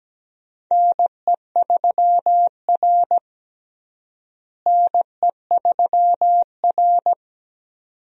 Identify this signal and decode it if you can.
Morse code, heard as a keyed tone, about 17 wpm, 700 Hz. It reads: NE3R NE3R
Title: USA Call Signs - 4 Characters - Repeated 2x - 17wpm